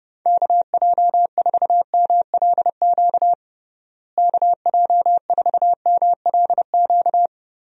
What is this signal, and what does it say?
Morse code, heard as a keyed tone, about 30 wpm, 700 Hz. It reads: KJ4MLQ KJ4MLQ
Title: USA Call Signs - 6 Characters - Repeated 3x - 30wpm